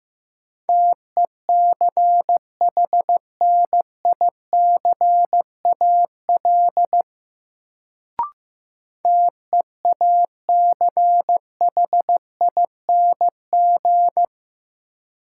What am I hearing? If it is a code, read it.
Morse code, heard as a keyed tone, about 15 wpm, 700 Hz. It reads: TECHNICAL TEACHING
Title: Sentences from Top 1100 Words - New Word Drill - 15wpm